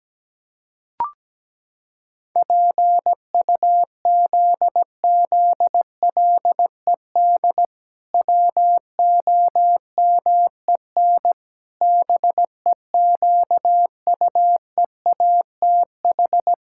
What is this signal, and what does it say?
Morse code, heard as a keyed tone, about 17 wpm, 700 Hz. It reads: PUZZLED WOMEN BEQUEATH
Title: Pangram Challenge v2 - 17wpm